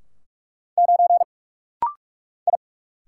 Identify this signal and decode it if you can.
Morse code, heard as a keyed tone, about 46 wpm, 700 Hz. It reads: 9 I
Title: Single Letter-Number - Rapid-Fire 45wpm